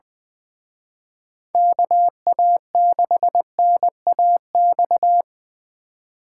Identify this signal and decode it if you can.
Morse code, heard as a keyed tone, about 25 wpm, 700 Hz. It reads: KA6NAX
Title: Speed Racing - USA 2x3 Call Signs 20wpm